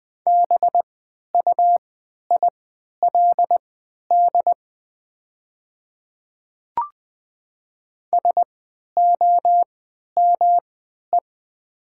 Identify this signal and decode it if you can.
Morse code, heard as a keyed tone, about 20 wpm, spaced as for 11 wpm, 700 Hz. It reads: BUILD SOME
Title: 3x Character Spacing - Top 200 Words - 20wpm